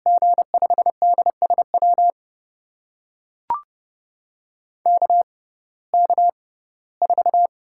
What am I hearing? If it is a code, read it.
Morse code, heard as a keyed tone, about 29 wpm, spaced as for 18 wpm, 700 Hz. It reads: G5DSW KK4
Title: USA Call Signs - 6 Characters - 6x Character Spacing - 30wpm